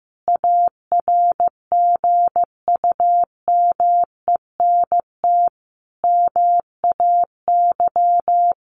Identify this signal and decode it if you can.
Morse code, heard as a keyed tone, about 15 wpm, 700 Hz. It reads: ARGUMENT MAY
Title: Sentences from Top 1400 Words 15wpm